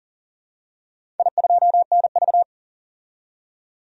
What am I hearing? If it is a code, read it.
Morse code, heard as a keyed tone, about 40 wpm, 700 Hz. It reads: I2NV
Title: DXCC-100 Call Signs - 4 Characters - 40wpm